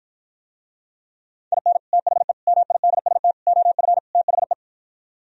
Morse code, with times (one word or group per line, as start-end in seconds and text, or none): 1.5-1.8 IN
1.9-2.3 THE
2.5-3.3 MIDST
3.5-4.0 OF
4.1-4.5 THE